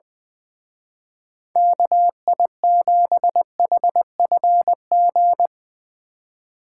Morse code, heard as a keyed tone, about 20 wpm, 700 Hz. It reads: KI7HFG
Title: Speed Racing - USA Call Signs 20wpm